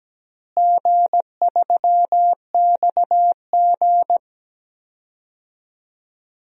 Morse code, heard as a keyed tone, about 17 wpm, 700 Hz. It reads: G3XG
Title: DXCC-100 Call Signs - 4 Characters - Repeated 3x - 17wpm